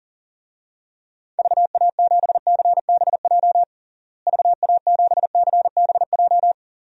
Morse code, heard as a keyed tone, about 37 wpm, 700 Hz. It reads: VA7CBJ VA7CBJ
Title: DXCC-100 Call Signs - 6 Characters - Repeated 3x - 40wpm